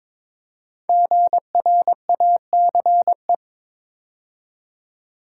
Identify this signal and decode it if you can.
Morse code, heard as a keyed tone, about 22 wpm, 700 Hz. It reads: GRACE
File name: 5 Letter Words - First 3 Letters Given - 22wpm